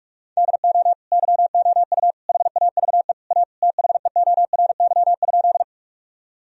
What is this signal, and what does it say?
Morse code, heard as a keyed tone, about 47 wpm, 700 Hz. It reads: DO YOU HAVE A THEORY?
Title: Sentences from Top 1000 Words - Review Part 3 - 45wpm